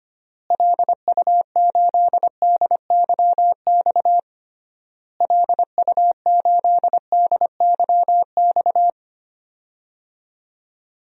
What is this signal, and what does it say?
Morse code, heard as a keyed tone, about 25 wpm, 700 Hz. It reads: LU8DYX LU8DYX
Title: DXCC-100 Call Signs - 6 Characters - Repeated 3x - 25wpm